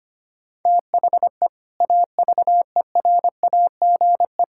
Morse code, heard as a keyed tone, about 25 wpm, 700 Hz. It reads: THE AVERAGE
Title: Sentences from Top 1000 Words - Review Part 6 - 25wpm